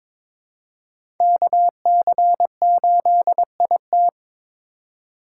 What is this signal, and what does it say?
Morse code, heard as a keyed tone, about 22 wpm, 700 Hz. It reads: KC8IT
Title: USA Call Signs - 5 Characters - 3x Character Spacing - 22wpm